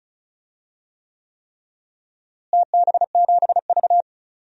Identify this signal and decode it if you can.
Morse code, heard as a keyed tone, about 36 wpm, 700 Hz. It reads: TB7V
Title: DXCC-100 Call Signs - 4 Characters - Repeated 3x - 35wpm